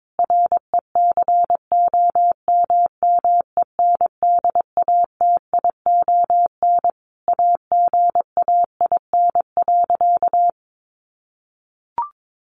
Morse code, heard as a keyed tone, about 22 wpm, 700 Hz. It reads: RECOMMENDATION AGAIN.
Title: Sentences from Top 1400 Words 22wpm